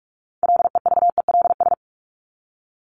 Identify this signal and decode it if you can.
Morse code, heard as a keyed tone, about 44 wpm, 700 Hz. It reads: LEVELS
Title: ES:S Suffix - Top 100 Words with Base Words 45wpm